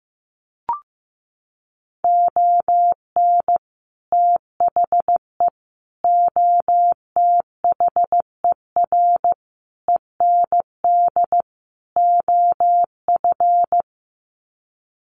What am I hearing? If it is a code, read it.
Morse code, heard as a keyed tone, about 15 wpm, 700 Hz. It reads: ON THE OTHER END OF
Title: Sets of 5 Words Encore 15wpm